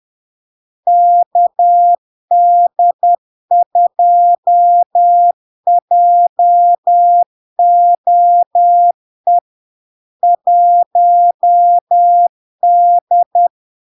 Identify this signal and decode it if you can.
Morse code, heard as a keyed tone, about 10 wpm, 700 Hz. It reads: KD2JOE 1D
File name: LARK-FD-2022-Log-in-CW-10wpm-700-Hz